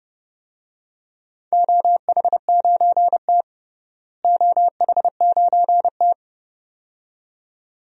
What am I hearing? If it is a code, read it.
Morse code, heard as a keyed tone, about 32 wpm, 700 Hz. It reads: OH9T OH9T
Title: DXCC-100 Call Signs - 4 Characters - Repeated 2x - 30wpm